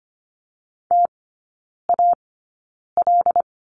Morse code, heard as a keyed tone, about 25 wpm, spaced as for 9 wpm, 700 Hz. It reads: TAL
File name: Lesson 17e - Letter u - Three Characters 25wpm